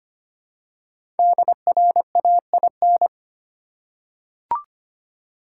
Morse code, heard as a keyed tone, about 25 wpm, 700 Hz. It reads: DRAIN
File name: 5 Letter Words - First 3 Letters Given - 25wpm